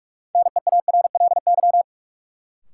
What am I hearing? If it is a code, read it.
Morse code, heard as a keyed tone, about 44 wpm, 700 Hz. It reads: NEARLY
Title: Sentences from Top 1300 Words - New Word Drill - 45wpm